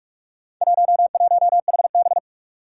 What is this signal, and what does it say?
Morse code, heard as a keyed tone, about 42 wpm, 700 Hz. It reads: 11HB
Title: Four Challenging Letter-Number Combinations 45wpm